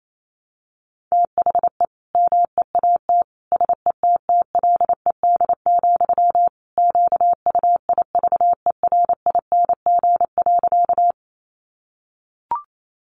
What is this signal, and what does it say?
Morse code, heard as a keyed tone, about 28 wpm, 700 Hz. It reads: THE MEAT SETTLED, QUIVERING.
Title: A Sound Of Thunder 28wpm